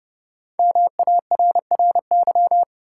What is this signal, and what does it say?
Morse code, heard as a keyed tone, about 32 wpm, 700 Hz. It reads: MARRY
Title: Sentences from Top 1200 Words - New Word Drill - 30wpm